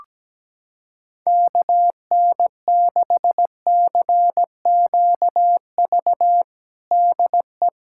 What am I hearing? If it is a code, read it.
Morse code, heard as a keyed tone, about 17 wpm, 700 Hz. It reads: KN6CQV DE
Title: USA Callsign DE Callsign - 6 Characters - 17wpm